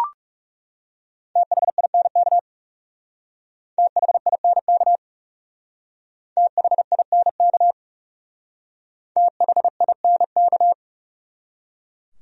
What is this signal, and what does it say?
Morse code, heard as a keyed tone, about 40 wpm, 700 Hz. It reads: THINK THINK THINK THINK